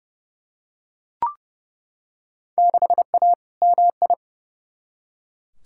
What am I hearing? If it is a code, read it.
Morse code, heard as a keyed tone, about 31 wpm, 700 Hz. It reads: 6A MI
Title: Field Day Exchanges - 30wpm